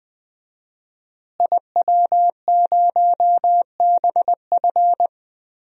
Morse code, heard as a keyed tone, about 20 wpm, 700 Hz. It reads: IW0BF
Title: DXCC-100 Call Signs - 5 Characters - Repeated 2x - 20wpm